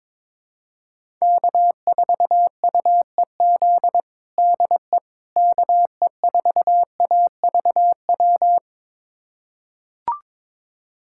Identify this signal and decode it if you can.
Morse code, heard as a keyed tone, about 22 wpm, 700 Hz. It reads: K4UEZ DE KE4AVW
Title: USA Callsign DE Callsign 22wpm